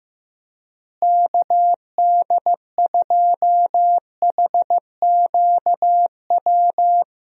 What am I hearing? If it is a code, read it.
Morse code, heard as a keyed tone, about 16 wpm, 700 Hz. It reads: KD2HQW
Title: USA Call Signs - 6 Characters - Repeated 2x - 15wpm